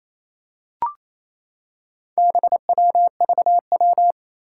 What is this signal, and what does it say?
Morse code, heard as a keyed tone, about 28 wpm, 700 Hz. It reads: BWVW
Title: Four Challenging Letter-Number Combinations 28wpm